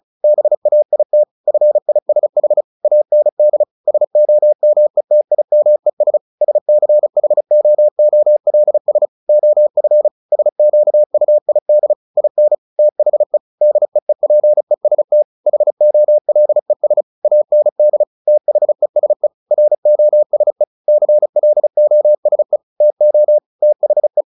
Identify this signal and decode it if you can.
Morse code, heard as a keyed tone, about 34 wpm, 600 Hz. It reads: BAIT FISH AND SOMETIMES SCHOOLS OF SQUID IN THE DEEPEST HOLES AND THESE ROSE CLOSE TO THE
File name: the old man and the sea / the old man and the sea 1 35wpm